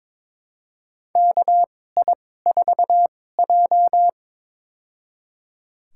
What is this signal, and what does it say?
Morse code, heard as a keyed tone, about 22 wpm, spaced as for 16 wpm, 700 Hz. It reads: KI4J
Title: USA Call Signs - 4 Characters - 2x Character Spacing - 22wpm